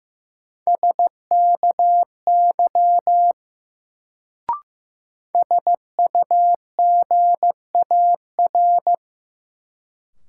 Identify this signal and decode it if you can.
Morse code, heard as a keyed tone, about 15 wpm, 700 Hz. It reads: SKY SUGAR
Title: Sentences from Top 1200 Words - New Word Drill - 15wpm